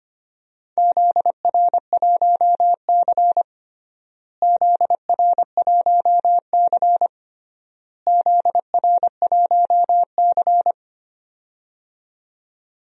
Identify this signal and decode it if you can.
Morse code, heard as a keyed tone, about 25 wpm, 700 Hz. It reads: ZR1C ZR1C ZR1C
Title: DXCC-100 Call Signs - 4 Characters - Repeated 3x - 25wpm